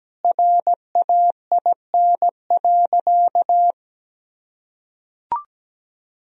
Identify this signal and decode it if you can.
Morse code, heard as a keyed tone, about 17 wpm, 700 Hz. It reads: RAIN.
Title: Sentences from Top 1000 Words - Review Part 1 - 17wpm